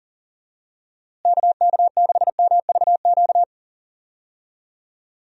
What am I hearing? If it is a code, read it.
Morse code, heard as a keyed tone, about 37 wpm, 700 Hz. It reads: KK6MVQ
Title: USA Call Signs - 6 Characters - Repeated 3x - 40wpm